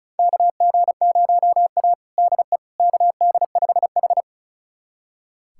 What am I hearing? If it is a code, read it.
Morse code, heard as a keyed tone, about 34 wpm, 700 Hz. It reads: KG0A DE KD5H
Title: USA Callsign DE Callsign - 4 Characters - 35wpm